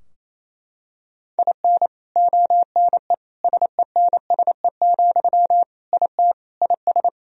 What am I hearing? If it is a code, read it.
Morse code, heard as a keyed tone, about 28 wpm, 700 Hz. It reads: IN ONE SENSE, IT IS
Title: Sentences from Top 1000 Words - Review Part 4 - 28wpm